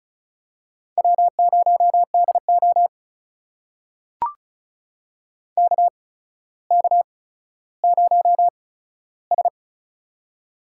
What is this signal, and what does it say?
Morse code, heard as a keyed tone, about 34 wpm, spaced as for 15 wpm, 700 Hz. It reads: W0DO KK0S
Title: USA Call Signs - 4 Characters - 8x Character Spacing - 35wpm